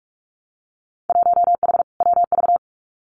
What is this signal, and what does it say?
Morse code, heard as a keyed tone, about 46 wpm, 700 Hz. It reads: 1H WV
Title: Winter Field Day Exchanges Encore 45wpm